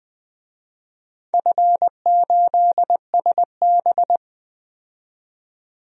Morse code, heard as a keyed tone, about 20 wpm, 700 Hz. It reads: F8SB
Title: DXCC-100 Call Signs - 4 Characters - 20wpm